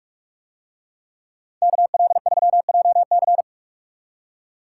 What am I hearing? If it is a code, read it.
Morse code, heard as a keyed tone, about 46 wpm, 700 Hz. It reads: KL3JC